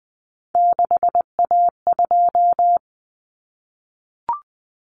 Morse code, heard as a keyed tone, about 20 wpm, 700 Hz. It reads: 6A2
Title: Common Cut Numbers - 3-Digit 20wpm